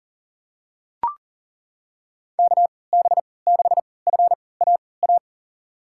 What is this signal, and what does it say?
Morse code, heard as a keyed tone, about 40 wpm, spaced as for 23 wpm, 700 Hz. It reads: KB6FAA
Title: USA Call Signs - 3x Character Spacing - 40wpm